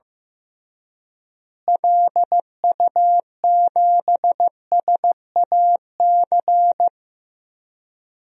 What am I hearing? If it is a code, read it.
Morse code, heard as a keyed tone, about 15 wpm, 700 Hz. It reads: LU7SAC
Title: DXCC-100 Call Signs - 6 Characters - Repeated 3x - 15wpm